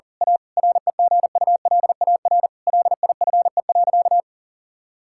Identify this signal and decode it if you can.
Morse code, heard as a keyed tone, about 40 wpm, 700 Hz. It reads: A REGULAR LIFE.
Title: Sentences from Top 1000 Words - Review Part 6 - 40wpm